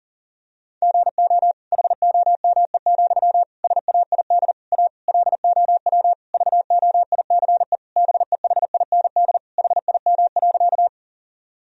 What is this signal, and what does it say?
Morse code, heard as a keyed tone, about 40 wpm, 700 Hz. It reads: GO HOME, SAID A LOW VOICE BEHIND HIM.
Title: Sentences from Top 1000 Words - Review Part 1 - 40wpm